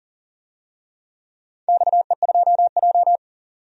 Morse code, heard as a keyed tone, about 40 wpm, 700 Hz. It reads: XE2J
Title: DXCC-100 Call Signs - 4 Characters - 40wpm